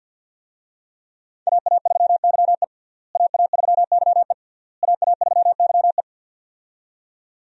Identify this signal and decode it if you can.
Morse code, heard as a keyed tone, about 51 wpm, 700 Hz. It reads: AA3YE AA3YE AA3YE
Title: USA Call Signs - 5 Characters - Repeated 3x - 50wpm